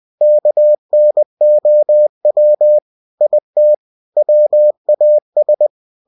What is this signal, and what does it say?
Morse code, heard as a keyed tone, about 20 wpm, 600 Hz. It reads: KNOW IT WAS